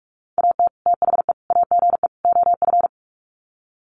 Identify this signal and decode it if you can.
Morse code, heard as a keyed tone, about 44 wpm, 700 Hz. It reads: AT THE AGE OF
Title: Sets of 4 Words 45wpm